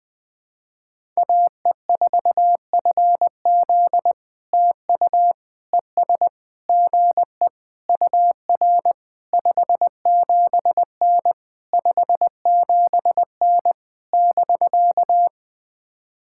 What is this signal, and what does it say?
Morse code, heard as a keyed tone, about 20 wpm, 700 Hz. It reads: AE4FZ TU ES GE UR 57N 57N <BK>
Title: POTA QSO Practice - 20wpm